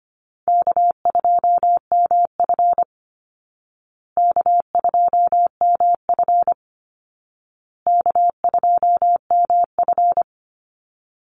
Morse code, heard as a keyed tone, about 25 wpm, 700 Hz. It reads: K2MF K2MF K2MF
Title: Easy Call Signs Repeated 3x - 25wpm